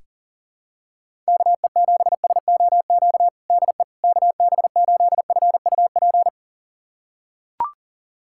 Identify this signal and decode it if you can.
Morse code, heard as a keyed tone, about 40 wpm, 700 Hz. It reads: KE7SOQ DE KB8FUP
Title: USA Callsign DE Callsign 40wpm